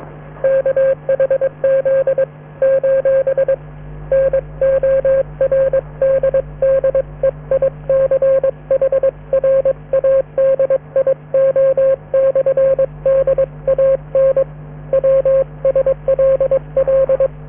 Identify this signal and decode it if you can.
Morse code, heard as a keyed tone, about 22 wpm, 540 Hz, with radio noise. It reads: KHZ : NORDDEICH RADIO/DAN WSLL